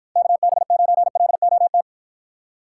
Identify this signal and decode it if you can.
Morse code, heard as a keyed tone, about 50 wpm, 700 Hz. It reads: KB9LOT